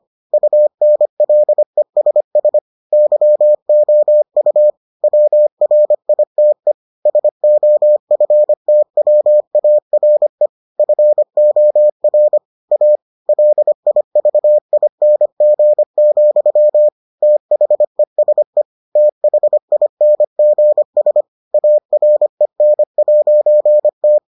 Morse code, heard as a keyed tone, about 25 wpm, 600 Hz. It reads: UNLESS YOU WRITE SOFTWARE FOR A LIVING, THESE THINGS AREN'T